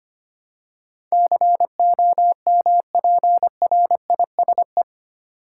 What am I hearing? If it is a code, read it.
Morse code, heard as a keyed tone, about 25 wpm, 700 Hz. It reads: COMPRISE
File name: Eight Letter Words - 25wpm